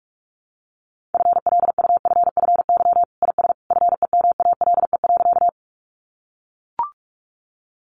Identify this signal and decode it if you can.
Morse code, heard as a keyed tone, about 44 wpm, 700 Hz. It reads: FLUFFY IS FEMALE.
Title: Easy Phrases 45wpm